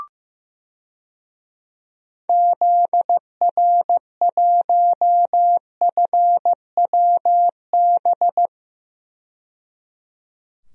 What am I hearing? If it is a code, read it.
Morse code, heard as a keyed tone, about 15 wpm, 700 Hz. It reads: ZR1FWB
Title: DXCC-100 Call Signs - 6 Characters - 15wpm